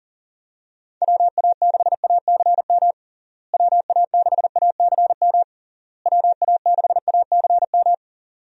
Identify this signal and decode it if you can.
Morse code, heard as a keyed tone, about 40 wpm, 700 Hz. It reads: WA6ACM WA6ACM WA6ACM
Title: USA Call Signs - 6 Characters - Repeated 3x - 40wpm